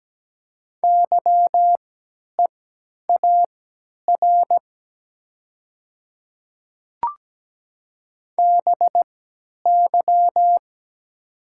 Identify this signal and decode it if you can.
Morse code, heard as a keyed tone, about 17 wpm, spaced as for 10 wpm, 700 Hz. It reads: YEAR BY
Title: 3x Character Spacing - Top 300 Words - 17wpm